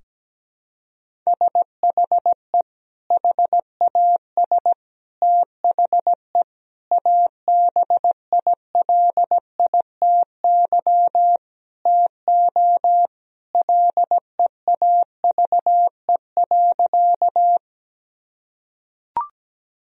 Morse code, heard as a keyed tone, about 17 wpm, 700 Hz. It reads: SHE HAS THE ABILITY TO LEAVE.
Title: Sentences from Top 1000 Words - Review Part 3 - 17wpm